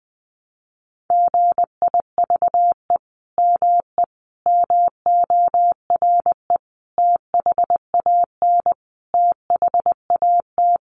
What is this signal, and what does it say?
Morse code, heard as a keyed tone, about 20 wpm, 700 Hz. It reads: GIVE ME MORE THAN THAT
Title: Sentences from Top 200 Words 20wpm